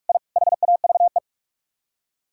Morse code, heard as a keyed tone, about 40 wpm, 700 Hz. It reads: I HAVE